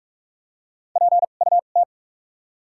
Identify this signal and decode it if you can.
Morse code, heard as a keyed tone, about 43 wpm, spaced as for 32 wpm, 700 Hz. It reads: PUT
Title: 2x Character Spacing - Top 300 Words - 45wpm